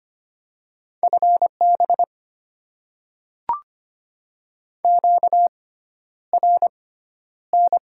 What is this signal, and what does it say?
Morse code, heard as a keyed tone, about 25 wpm, spaced as for 11 wpm, 700 Hz. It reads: FB QRN
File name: Top 90 QSO Elements - 6x Character Spacing - 25wpm